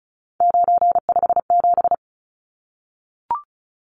Morse code, heard as a keyed tone, about 34 wpm, 700 Hz. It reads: 957
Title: Numbers - 3-Digit 35wpm